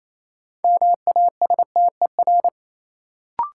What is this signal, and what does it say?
Morse code, heard as a keyed tone, about 27 wpm, 700 Hz. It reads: MASTER
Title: Sentences from Top 1300 Words - New Word Drill - 28wpm